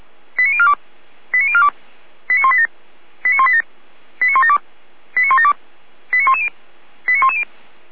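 Five-tone selective call (ZVEI), heard as quick runs of five tones, tones of about 70 ms. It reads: 89042 89042 89187 89187 89172 89172 89110 89110
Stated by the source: ZVEI five-tone selective call received on SDR.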